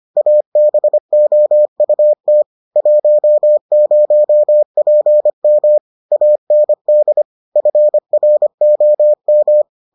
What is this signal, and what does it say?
Morse code, heard as a keyed tone, about 25 wpm, 600 Hz. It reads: ABOUT 10PM AND FROM